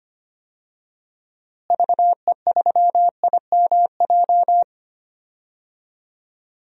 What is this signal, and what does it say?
Morse code, heard as a keyed tone, about 25 wpm, 700 Hz. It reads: VE3IMJ
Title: DXCC-100 Call Signs - 6 Characters - Repeated 2x - 25wpm